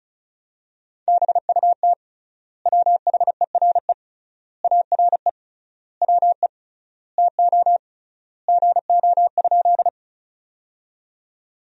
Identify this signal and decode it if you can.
Morse code, heard as a keyed tone, about 36 wpm, 700 Hz. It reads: BUT WHERE ARE WE TO GO?
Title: 3x Word-Spacing - Sentences from Top 200 Words - 35wpm